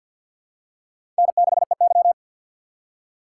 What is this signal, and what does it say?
Morse code, heard as a keyed tone, about 48 wpm, 700 Hz. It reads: N6EY
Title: USA Call Signs - 4x Character Spacing - 50wpm